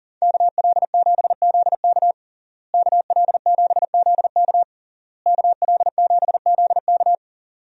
Morse code, heard as a keyed tone, about 40 wpm, 700 Hz. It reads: KL7ZK KL7ZK KL7ZK
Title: USA Call Signs - 5 Characters - Repeated 3x - 40wpm